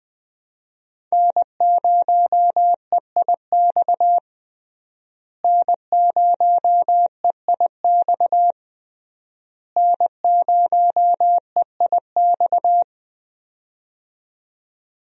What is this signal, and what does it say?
Morse code, heard as a keyed tone, about 20 wpm, 700 Hz. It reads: N0EIX N0EIX N0EIX
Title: USA Call Signs - 5 Characters - Repeated 3x - 20wpm